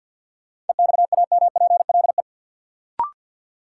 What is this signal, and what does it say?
Morse code, heard as a keyed tone, about 52 wpm, 700 Hz. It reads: EXAMPLE
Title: Sentences from Top 400 Words - New Word Drill - 50wpm